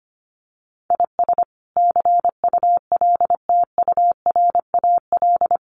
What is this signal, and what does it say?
Morse code, heard as a keyed tone, about 25 wpm, 700 Hz. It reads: IS CULTURAL
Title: Sentences from Top 1500 Words 25wpm